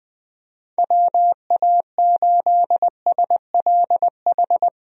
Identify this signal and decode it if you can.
Morse code, heard as a keyed tone, about 20 wpm, 700 Hz. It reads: WA8SLH
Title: USA Call Signs - 6 Characters - Repeated 3x - 20wpm